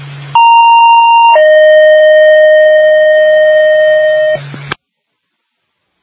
Two-tone page, heard as a one-second tone, then a three-second tone, about 930 then 620 Hz, with noise. It is heard from 0.3 to 4.4 s.